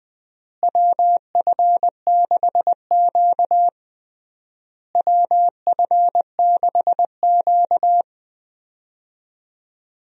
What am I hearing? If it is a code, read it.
Morse code, heard as a keyed tone, about 20 wpm, 700 Hz. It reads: WF6Q WF6Q
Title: USA Call Signs - 4 Characters - Repeated 3x - 20wpm